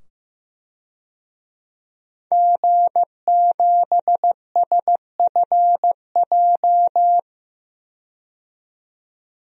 Morse code, heard as a keyed tone, about 16 wpm, 700 Hz. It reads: G7SFJ
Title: DXCC-100 Call Signs - 5 Characters - 15wpm